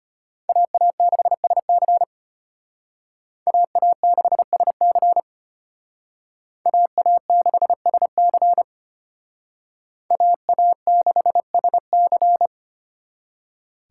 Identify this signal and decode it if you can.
Morse code, heard as a keyed tone, about 32 wpm, 700 Hz. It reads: AA6SC AA6SC AA6SC AA6SC